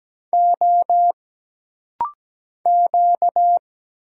Morse code, heard as a keyed tone, about 17 wpm, 700 Hz. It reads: O Q
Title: Single Letter-Number Alt - Rapid-Fire 17wpm